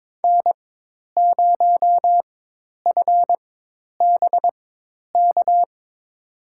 Morse code, heard as a keyed tone, about 21 wpm, spaced as for 10 wpm, 700 Hz. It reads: N0FBK
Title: USA Call Signs - 4x Character Spacing - 22wpm